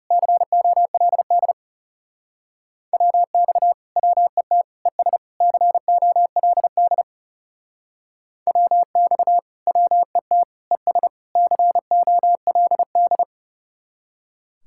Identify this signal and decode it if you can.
Morse code, heard as a keyed tone, about 37 wpm, 700 Hz. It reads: COLD WX WET ES COLD WX WET ES COLD
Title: Speed-Racing Common QSO Phrases Encore - 30wpm